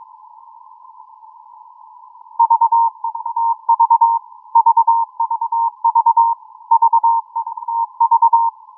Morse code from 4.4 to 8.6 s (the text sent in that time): VVV V TV